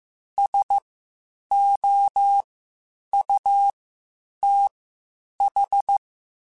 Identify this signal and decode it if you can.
Morse code, heard as a keyed tone, about 15 wpm, spaced as for 8 wpm, 800 Hz. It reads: SOUTH
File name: (spaced) / QSOs Spaced-mp3s / kc7haa @13 sp